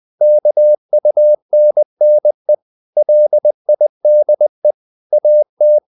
Morse code, heard as a keyed tone, about 20 wpm, 600 Hz. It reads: KUNNE LIDE AT
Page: Ferie 20 wpm